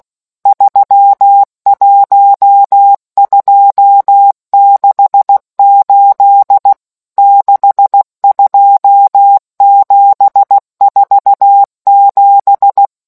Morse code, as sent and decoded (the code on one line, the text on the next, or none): ...-- .---- ..--- -.... ---.. / -.... ..--- --... ....- --...
31268 62747